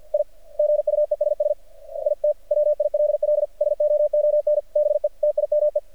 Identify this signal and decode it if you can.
Morse code, heard as a keyed tone, about 48 wpm, 600 Hz. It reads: I GUESS IT WILL SOON BE TIME
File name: archived / CRSnet